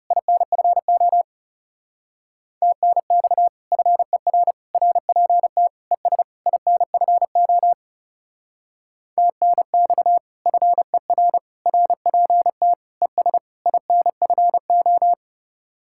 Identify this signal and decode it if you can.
Morse code, heard as a keyed tone, about 38 wpm, 700 Hz. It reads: INFO TNX FER RPT ES INFO TNX FER RPT ES INFO